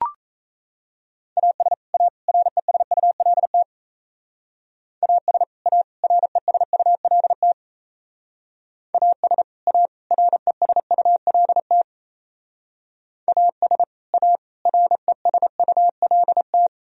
Morse code, heard as a keyed tone, about 34 wpm, 700 Hz. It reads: AS A RESULT AS A RESULT AS A RESULT AS A RESULT
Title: Speed-Racing - Sets of 3 Words 28wpm